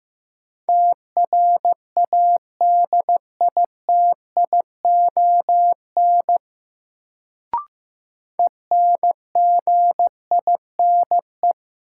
Morse code, heard as a keyed tone, about 15 wpm, 700 Hz. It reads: TRADITION ENGINE
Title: Sentences from Top 1500 Words - New Word Drill - 15wpm